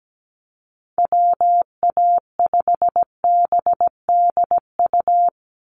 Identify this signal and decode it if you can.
Morse code, heard as a keyed tone, about 17 wpm, 700 Hz. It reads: WA5BDU
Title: Call Signs Repeated 2x - 17wpm